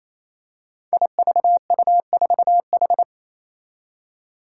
Morse code, heard as a keyed tone, about 28 wpm, 700 Hz. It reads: IVU4H
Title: Five Challenging Letter-Number Combinations 28wpm